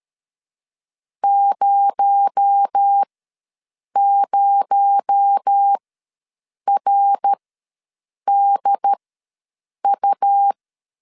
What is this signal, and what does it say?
Morse code, heard as a keyed tone, about 13 wpm, spaced as for 7 wpm, 800 Hz. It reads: 00RDU
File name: LEZIONI / 6 PKX0 / 6 PKX0 ws 12wpm 3448